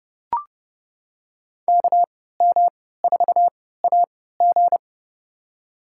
Morse code, heard as a keyed tone, about 30 wpm, spaced as for 17 wpm, 700 Hz. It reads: KM4AG
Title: USA Call Signs - 5 Characters - 3x Character Spacing - 30wpm